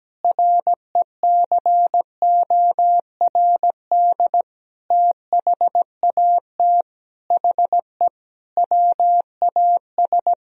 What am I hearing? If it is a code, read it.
Morse code, heard as a keyed tone, about 17 wpm, 700 Hz. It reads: RECORD THAT HE WAS
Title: Sentences from Top 500 Words and Abbreviations 17wpm